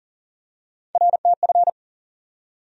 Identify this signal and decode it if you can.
Morse code, heard as a keyed tone, about 40 wpm, 700 Hz. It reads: RTF
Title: USA Call Sign Postfixes - 3 Letters 40wpm